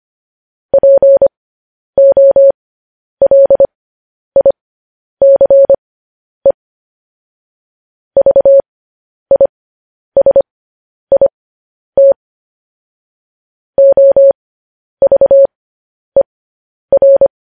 Morse code, heard as a keyed tone, about 25 wpm, spaced as for 10 wpm, 550 Hz. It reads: POLICE VISIT OVER
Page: news-headlines-in-morse-code-at-10-wpm